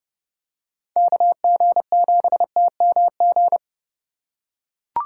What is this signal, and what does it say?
Morse code, heard as a keyed tone, about 30 wpm, 700 Hz. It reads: KG7TMG
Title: USA Call Signs - 6 Characters - 4x Character Spacing - 30wpm